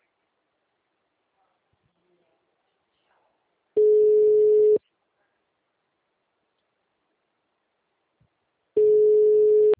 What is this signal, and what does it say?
Ringback tone (Europe, 425 Hz).